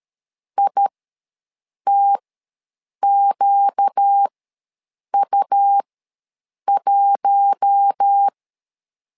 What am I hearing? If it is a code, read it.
Morse code, heard as a keyed tone, about 13 wpm, spaced as for 7 wpm, 800 Hz. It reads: ITQU1
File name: LEZIONI / 4 Q1 ws 12wpm 2129